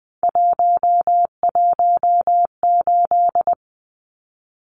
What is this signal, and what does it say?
Morse code, heard as a keyed tone, about 20 wpm, 700 Hz. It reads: 118
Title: Numbers - 3-Digit 20wpm